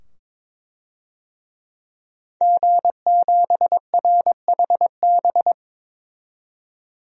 Morse code, heard as a keyed tone, about 22 wpm, 700 Hz. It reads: G7RHB
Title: DXCC-100 Call Signs - 5 Characters - 22wpm